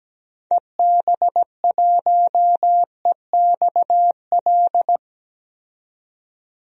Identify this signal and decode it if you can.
Morse code, heard as a keyed tone, about 17 wpm, 700 Hz. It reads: EB1EXL
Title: DXCC-100 Call Signs - 6 Characters - 17wpm